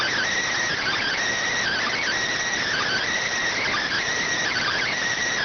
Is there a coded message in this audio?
Yes: Emergency Alert System (SAME) tones from 0.7 to 4.9 s, heard as harsh buzzing bursts.